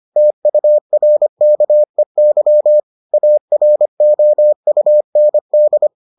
Morse code, heard as a keyed tone, about 25 wpm, 600 Hz. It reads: TURKEY AROUND